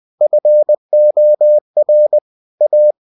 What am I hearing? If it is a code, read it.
Morse code, heard as a keyed tone, about 20 wpm, 600 Hz. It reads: FOR A
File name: cw-books / winnie-the-pooh / winnie-20wpm-page-0076